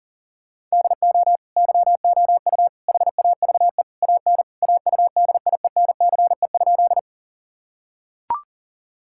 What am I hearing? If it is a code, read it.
Morse code, heard as a keyed tone, about 40 wpm, 700 Hz. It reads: DO YOU HAVE AN AUDIENCE?
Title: Sentences from Top 1000 Words - Review Part 5 - 40wpm